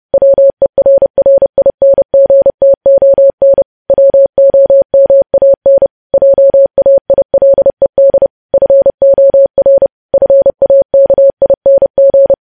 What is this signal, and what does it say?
Morse code, heard as a keyed tone, about 30 wpm, 550 Hz. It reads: WERRINGTON WOMAN JAILED FOR FAKING
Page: news headlines in morse code at 30 wpm